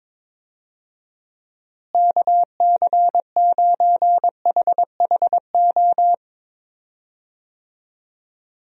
Morse code, heard as a keyed tone, about 22 wpm, 700 Hz. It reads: KC9HHO
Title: USA Call Signs - 6 Characters - Repeated 2x - 22wpm